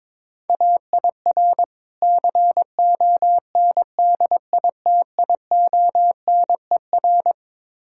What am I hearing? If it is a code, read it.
Morse code, heard as a keyed tone, about 23 wpm, 690 Hz. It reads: AIR CONDITIONER
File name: Easy Words - Speed-Racing with Hints - 22wpm